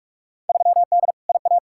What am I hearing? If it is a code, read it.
Morse code, heard as a keyed tone, about 42 wpm, 700 Hz. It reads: 3D IA
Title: Field Day Exchanges Encore - 45wpm